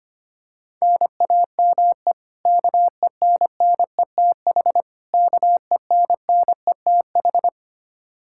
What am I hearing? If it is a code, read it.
Morse code, heard as a keyed tone, about 25 wpm, 700 Hz. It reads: NAME KENNETH KENNETH
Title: Practice QSO - Exchange 1 - Beginning - 25wpm